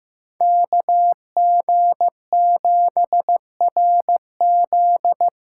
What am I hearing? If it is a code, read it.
Morse code, heard as a keyed tone, about 16 wpm, 700 Hz. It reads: KG7RZ
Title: USA Callsign DE Callsign 15wpm